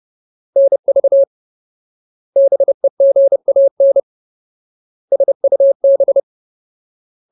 Morse code, heard as a keyed tone, about 30 wpm, 550 Hz. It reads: NV BEGAN SUB
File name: Ultimate Word List / 30WPM Ultimate